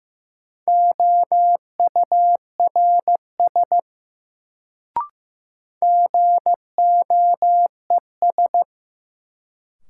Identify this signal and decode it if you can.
Morse code, heard as a keyed tone, about 15 wpm, 700 Hz. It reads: OURS GOES
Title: Sentences from Top 1000 Words - Review Part 5 - New Word Drill - 15wpm